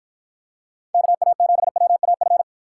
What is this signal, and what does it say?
Morse code, heard as a keyed tone, about 53 wpm, 700 Hz. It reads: KA7WAF